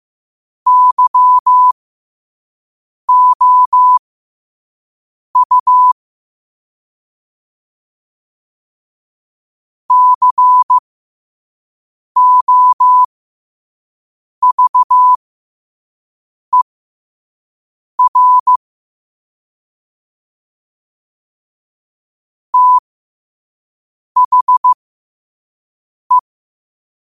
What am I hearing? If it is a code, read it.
Morse code, heard as a keyed tone, about 15 wpm, spaced as for 5 wpm, 1000 Hz. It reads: YOU COVER THE
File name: qotdfiles / 5wpm